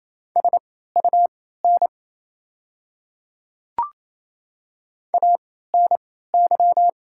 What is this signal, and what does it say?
Morse code, heard as a keyed tone, about 28 wpm, spaced as for 16 wpm, 690 Hz. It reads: SUN ANY
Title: 3x Character Spacing - Top 500 Words - 28wpm